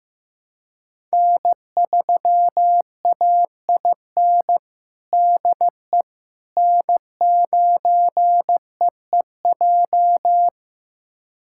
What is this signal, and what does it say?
Morse code, heard as a keyed tone, about 15 wpm, 700 Hz. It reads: N3AIN DE N9EEJ
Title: USA Callsign DE Callsign 15wpm